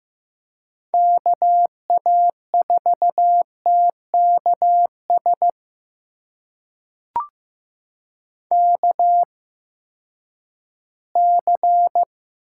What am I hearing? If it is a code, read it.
Morse code, heard as a keyed tone, about 15 wpm, spaced as for 10 wpm, 700 Hz. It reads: KA4TKS KC
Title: USA Call Signs - 6 Characters - 8x Character Spacing - 15wpm